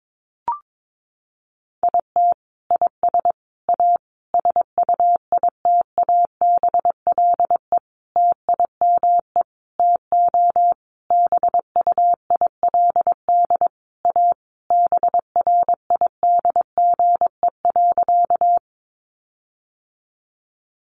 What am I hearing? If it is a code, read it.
Morse code, heard as a keyed tone, about 22 wpm, 700 Hz. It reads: IT IS A SUITABLE TIME TO BUILD A BRIDGE.
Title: Sentences from Top 1400 Words 22wpm